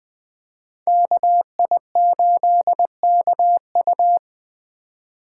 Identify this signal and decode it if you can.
Morse code, heard as a keyed tone, about 20 wpm, 700 Hz. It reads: KI8KU
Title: USA Call Signs - 5 Characters - 2x Character Spacing - 20wpm